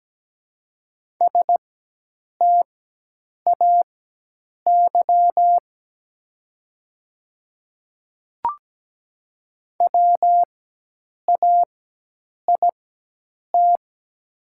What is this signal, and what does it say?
Morse code, heard as a keyed tone, about 17 wpm, spaced as for 8 wpm, 700 Hz. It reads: STAY WAIT
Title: 4x Character Spacing - Top 500 Words - 17wpm